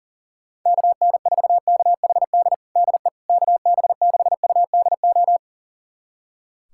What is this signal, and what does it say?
Morse code, heard as a keyed tone, about 40 wpm, 700 Hz. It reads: KN4KHD DE KB6UDO
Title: USA Callsign DE Callsign 40wpm